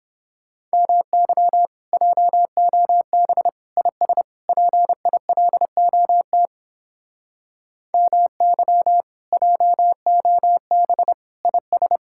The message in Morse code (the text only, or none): MY JOB IS PILOT MY JOB IS